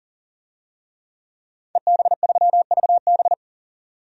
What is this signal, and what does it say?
Morse code, heard as a keyed tone, about 40 wpm, 700 Hz. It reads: EB3VB
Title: DXCC-100 Call Signs - 5 Characters - 40wpm